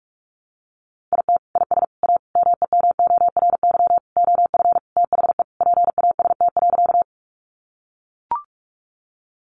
Morse code, heard as a keyed tone, about 44 wpm, 700 Hz. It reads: IT IS A MEMORY OF THE PAST.
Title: Sentences from Top 1300 Words 45wpm